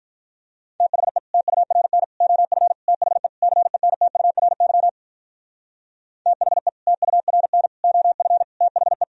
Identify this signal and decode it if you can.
Morse code, heard as a keyed tone, about 42 wpm, 700 Hz. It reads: THE TURN OF THE CENTURY THE TURN OF THE
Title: Speed-Racing - Sets of 5 Words 35wpm